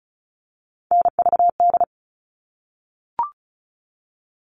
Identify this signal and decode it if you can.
Morse code, heard as a keyed tone, about 34 wpm, 700 Hz. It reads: NVD
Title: All Cut Numbers - 3-Digit 35wpm